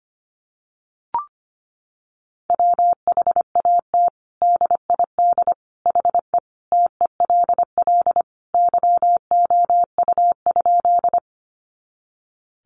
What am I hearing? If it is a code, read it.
Morse code, heard as a keyed tone, about 25 wpm, 700 Hz. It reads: WHAT DID HE TELL YOU?
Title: Sentences from Top 200 Words 25wpm